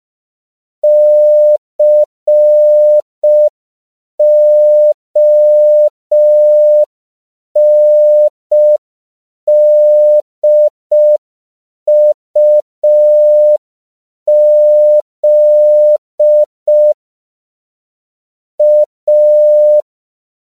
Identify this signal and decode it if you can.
Morse code, heard as a keyed tone, about 5 wpm, 600 Hz. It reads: CONDUZ A
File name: CW / 5PPM